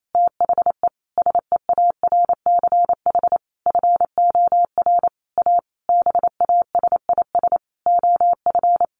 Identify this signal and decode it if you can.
Morse code, heard as a keyed tone, about 28 wpm, 700 Hz. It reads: THE SEARCH FOR A BASIS OF
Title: Sentences from Top 1500 Words 28wpm